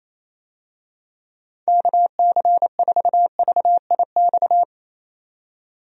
Morse code, heard as a keyed tone, about 28 wpm, 700 Hz. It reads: KC4VIX
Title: USA Call Signs - 6 Characters - 28wpm